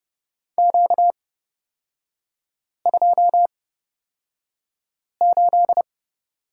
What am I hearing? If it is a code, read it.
Morse code, heard as a keyed tone, about 30 wpm, spaced as for 5 wpm, 700 Hz. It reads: Q28
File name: Lesson 38e - Letter q - Three Characters 30wpm